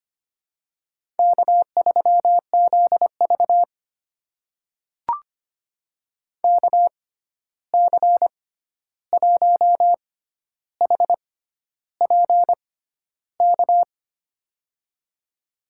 Morse code, heard as a keyed tone, about 25 wpm, spaced as for 11 wpm, 700 Hz. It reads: K3ZV KC1HPK
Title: USA Call Signs - 6x Character Spacing - 25wpm